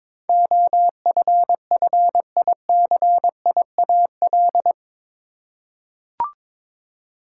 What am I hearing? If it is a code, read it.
Morse code, heard as a keyed tone, about 22 wpm, 700 Hz. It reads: OFFICIAL
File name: practice / 22wpm / Eight Letter Words - 22wpm